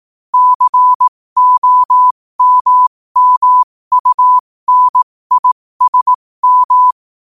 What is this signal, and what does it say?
Morse code, heard as a keyed tone, about 18 wpm, spaced as for 15 wpm, 1000 Hz. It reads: COMMUNISM
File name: qotdfiles / 15wpm